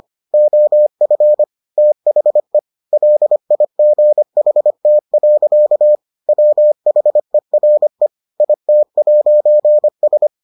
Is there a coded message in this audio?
Yes: Morse code from 0.3 to 10.3 s, heard as a keyed tone, about 25 wpm, 600 Hz. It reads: OF THE LIGHT. WHERE IT'S